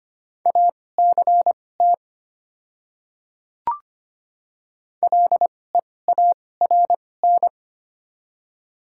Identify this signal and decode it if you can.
Morse code, heard as a keyed tone, about 25 wpm, spaced as for 18 wpm, 700 Hz. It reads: ACT LEARN
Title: 2x Character Spacing - Top 500 Words - 25wpm